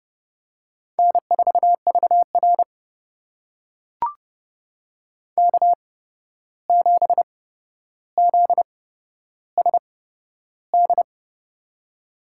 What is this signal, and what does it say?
Morse code, heard as a keyed tone, about 31 wpm, spaced as for 12 wpm, 700 Hz. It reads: N4VR K7ZSD
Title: USA Call Signs - 8x Character Spacing - 30wpm